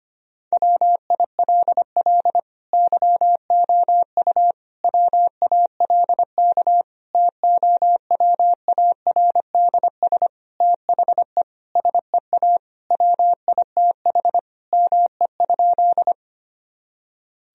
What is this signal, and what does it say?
Morse code, heard as a keyed tone, about 25 wpm, 700 Hz. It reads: WILL YOU WALK TOWARDS THE SEA WITH ME?
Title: Sentences from Top 1000 Words - Review Part 2 - 25wpm